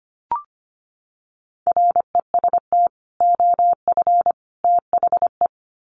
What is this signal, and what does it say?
Morse code, heard as a keyed tone, about 25 wpm, 700 Hz. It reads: REST OF THE
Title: Sets of 3 Words 25wpm